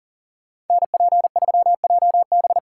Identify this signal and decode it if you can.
Morse code, heard as a keyed tone, about 48 wpm, 700 Hz. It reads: NP3JB
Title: Speed Racing - USA Call Signs 40wpm